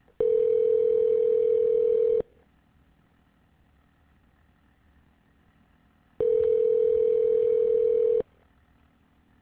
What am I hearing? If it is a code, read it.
Ringback tone (North America), 440 + 480 Hz.